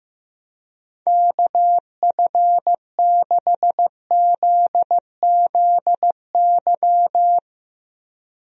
Morse code, heard as a keyed tone, about 15 wpm, 700 Hz. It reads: KF6ZZY
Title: USA Call Signs - 6 Characters - Repeated 3x - 15wpm